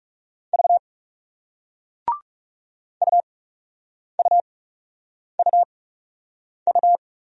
Morse code, heard as a keyed tone, about 39 wpm, spaced as for 8 wpm, 700 Hz. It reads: V UUUU